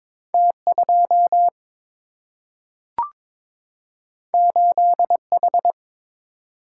Morse code, heard as a keyed tone, about 22 wpm, 700 Hz. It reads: T 2 8 H
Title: Single-Letter Single-Number 22wpm